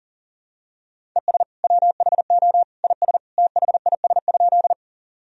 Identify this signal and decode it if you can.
Morse code, heard as a keyed tone, about 40 wpm, 700 Hz. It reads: ES WHO IS THIS?
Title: Sentences from Top 100 Words and Abbreviations 40wpm